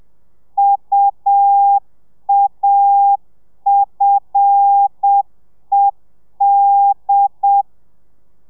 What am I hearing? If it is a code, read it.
Morse code, heard as a keyed tone, about 7 wpm, 800 Hz. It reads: UAFED
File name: telegrafia / i2tfj / Lez5 30